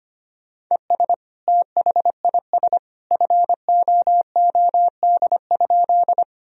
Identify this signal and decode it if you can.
Morse code, heard as a keyed tone, about 25 wpm, 700 Hz. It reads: ES THIS FOOD?
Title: Sentences from Top 300 Words and Abbreviations 25wpm